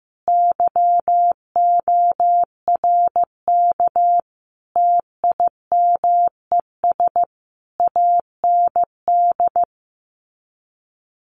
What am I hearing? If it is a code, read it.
Morse code, heard as a keyed tone, about 15 wpm, 700 Hz. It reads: YORK TIMES AND
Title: Sets of 5 Words Encore 15wpm